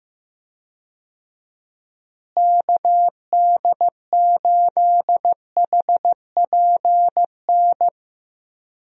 Morse code, heard as a keyed tone, about 15 wpm, 700 Hz. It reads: KD8HPN
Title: USA Call Signs - 6 Characters - Repeated 3x - 15wpm